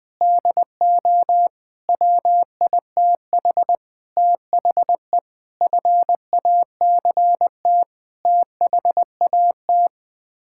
Morse code, heard as a keyed tone, about 20 wpm, 700 Hz. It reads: DO WITH THE FACT THAT
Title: Sets of 8 Words - 20wpm